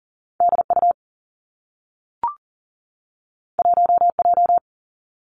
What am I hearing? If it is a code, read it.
Morse code, heard as a keyed tone, about 42 wpm, 700 Hz. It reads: D U 1 J
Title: Two Challenging Letter-Number Combinations 40wpm